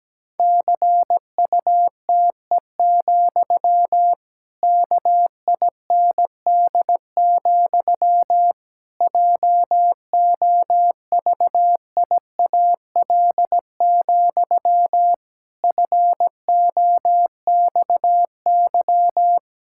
Morse code, heard as a keyed tone, about 17 wpm, 700 Hz. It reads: CUTE, KIND, JOVIAL, FOXY
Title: Pangram Challenge v2 - 17wpm